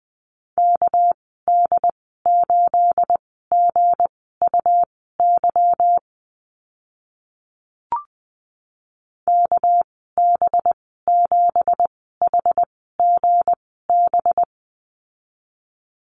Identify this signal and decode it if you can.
Morse code, heard as a keyed tone, about 20 wpm, spaced as for 14 wpm, 700 Hz. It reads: KD8GUY KB7HGB
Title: 245 Lesson 39g - Letter z - Call Signs 20wpm